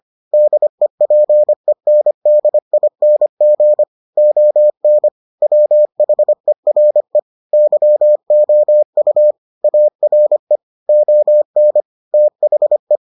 Morse code, heard as a keyed tone, about 25 wpm, 600 Hz. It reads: DEPENDING ON WHERE YOU ARE ON THE